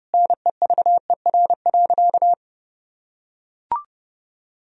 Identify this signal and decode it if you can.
Morse code, heard as a keyed tone, about 30 wpm, 700 Hz. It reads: NEVER.
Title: Sentences from Top 400 Words and Abbreviations 30wpm